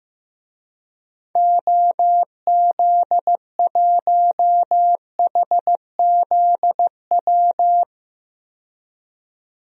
Morse code, heard as a keyed tone, about 15 wpm, 700 Hz. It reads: OZ1HZW
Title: DXCC-100 Call Signs - 6 Characters - 15wpm